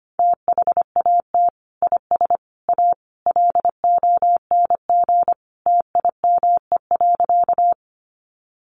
Morse code, heard as a keyed tone, about 25 wpm, 700 Hz. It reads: THAT IS A LONG TIME.
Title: Sentences from Top 100 Words 25wpm